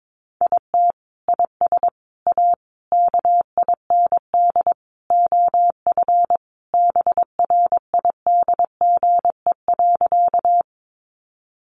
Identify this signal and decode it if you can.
Morse code, heard as a keyed tone, about 22 wpm, 700 Hz. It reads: IT IS A KIND OF BRIDGE.
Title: Sentences from Top 1200 Words 22wpm